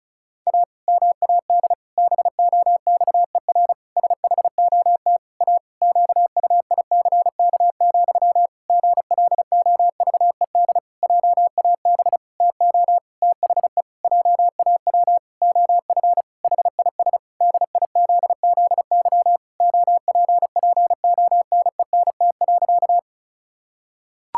Morse code, heard as a keyed tone, about 36 wpm, 700 Hz. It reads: A MAD BOXER SHOT A QUICK, GLOVED JAB TO THE JAW OF HIS DIZZY OPPONENT.
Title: Pangram Challenge v2 - 35wpm